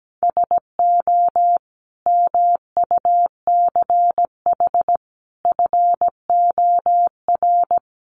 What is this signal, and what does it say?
Morse code, heard as a keyed tone, about 17 wpm, 700 Hz. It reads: SO MUCH FOR
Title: Sets of 5 Words Encore 17wpm